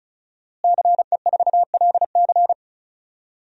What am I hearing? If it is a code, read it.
Morse code, heard as a keyed tone, about 36 wpm, 700 Hz. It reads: CE4LC
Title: DXCC-100 Call Signs - 5 Characters - Repeated 2x - 35wpm